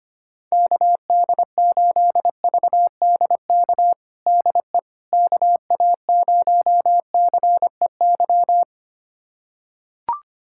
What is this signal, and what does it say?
Morse code, heard as a keyed tone, about 25 wpm, 700 Hz. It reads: KD8VDK DE KA0CEY
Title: USA Callsign DE Callsign 25wpm